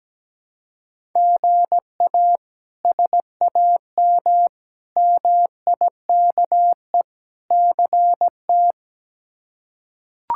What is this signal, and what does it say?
Morse code, heard as a keyed tone, about 17 wpm, 700 Hz. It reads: GA SAM MIKE CT
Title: K1USN SST QSO Practice - 17wpm